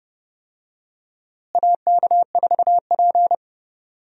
Morse code, heard as a keyed tone, about 30 wpm, 700 Hz. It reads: AK4P
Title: USA Call Signs - 4 Characters - Repeated 2x - 30wpm